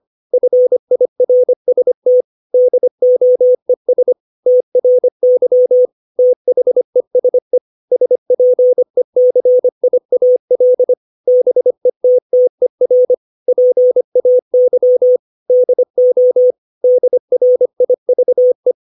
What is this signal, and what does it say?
Morse code, heard as a keyed tone, about 25 wpm, 500 Hz. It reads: FIRST DOES TRY THESE SPECIAL BETTER PAY DO DRIVE
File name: 632 words / 632 Words 25WPM